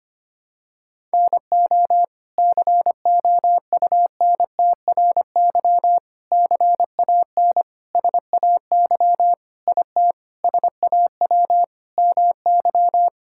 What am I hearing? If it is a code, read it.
Morse code, heard as a keyed tone, about 25 wpm, 700 Hz. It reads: NO COUNTRY CAN SAY IT SAW MY
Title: Sentences from Top 1000 Words - Review Part 5 - 25wpm